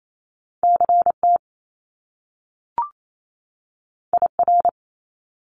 Morse code, heard as a keyed tone, about 28 wpm, 700 Hz. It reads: CT IR
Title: Top Two Letter Combinations in Words 28wpm